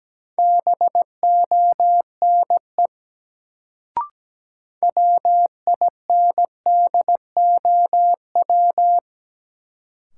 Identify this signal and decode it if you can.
Morse code, heard as a keyed tone, about 17 wpm, 700 Hz. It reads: BONE WINDOW
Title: Sentences from Top 1600 Words - New Word Drill - 17wpm